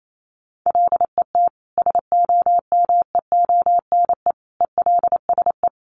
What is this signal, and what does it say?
Morse code, heard as a keyed tone, about 28 wpm, 700 Hz. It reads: LET SOMEONE ELSE
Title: Sentences from Top 1200 Words 28wpm